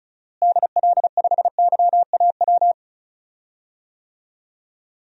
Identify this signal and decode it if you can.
Morse code, heard as a keyed tone, about 36 wpm, 700 Hz. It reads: DL5YAW
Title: DXCC-100 Call Signs - 6 Characters - Repeated 3x - 35wpm